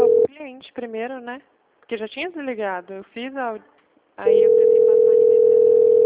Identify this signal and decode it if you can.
Ringback tone (North America), 440 + 480 Hz, with speech.
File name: out- -2002- - - .10860